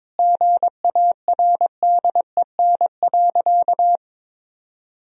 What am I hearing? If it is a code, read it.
Morse code, heard as a keyed tone, about 22 wpm, 700 Hz. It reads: GARDEN.
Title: Sentences from Top 1000 Words - Review Part 4 - 22wpm